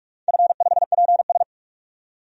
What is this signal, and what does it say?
Morse code, heard as a keyed tone, about 45 wpm, 700 Hz. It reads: F5PS
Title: DXCC-100 Call Signs - 4 Characters - 45wpm